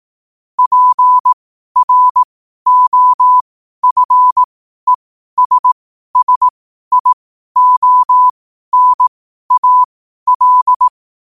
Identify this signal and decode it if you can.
Morse code, heard as a keyed tone, about 18 wpm, spaced as for 12 wpm, 1000 Hz. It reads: PROFESSIONAL
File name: qotdfiles / 12wpm